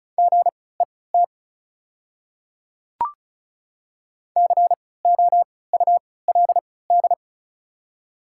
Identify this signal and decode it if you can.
Morse code, heard as a keyed tone, about 36 wpm, spaced as for 20 wpm, 690 Hz. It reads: GET COULD
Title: 3x Character Spacing - Top 500 Words - 35wpm